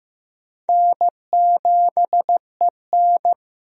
Morse code, heard as a keyed tone, about 15 wpm, 700 Hz. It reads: N7EN
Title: USA Call Signs - 4 Characters - Repeated 3x - 15wpm